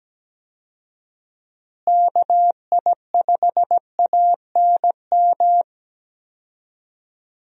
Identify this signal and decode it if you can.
Morse code, heard as a keyed tone, about 17 wpm, 700 Hz. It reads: KI5ANM
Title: USA Call Signs - 6 Characters - Repeated 3x - 17wpm